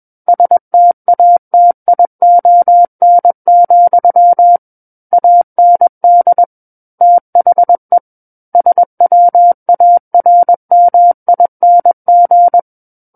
Morse code, heard as a keyed tone, about 21 wpm, 700 Hz. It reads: STATION, AND THE SWARMING
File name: Book One / 12-Chapter 12 - 21 WPM